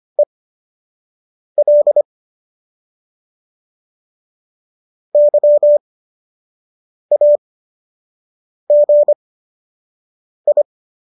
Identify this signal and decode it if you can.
Morse code, heard as a keyed tone, about 25 wpm, spaced as for 6 wpm, 590 Hz. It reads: EL YAGI